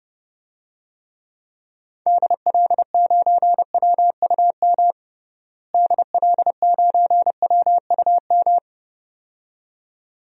Morse code, heard as a keyed tone, about 30 wpm, 700 Hz. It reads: DL9WUM DL9WUM
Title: DXCC-100 Call Signs - 6 Characters - Repeated 2x - 30wpm